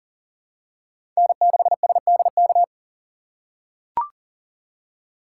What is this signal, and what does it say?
Morse code, heard as a keyed tone, about 40 wpm, 700 Hz. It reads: N6SDK
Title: USA Call Signs - 5 Characters - 3x Character Spacing - 40wpm